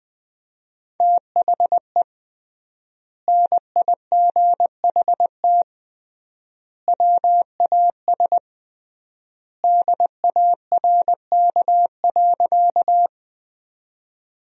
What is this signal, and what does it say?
Morse code, heard as a keyed tone, about 21 wpm, 700 Hz. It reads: THE NIGHT WAS DARK.
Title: 3x Word-Spacing - Sentences from Top 500 Words - 20wpm